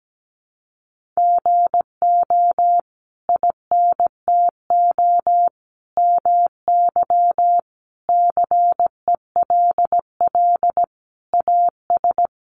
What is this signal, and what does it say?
Morse code, heard as a keyed tone, about 17 wpm, 700 Hz. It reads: GO INTO MY CELL AS
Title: Sentences from Top 1100 Words 17wpm